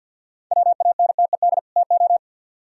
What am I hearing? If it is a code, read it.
Morse code, heard as a keyed tone, about 61 wpm, 700 Hz. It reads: WANTED TO
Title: Speed-Racing - Sets of 2 Words 50wpm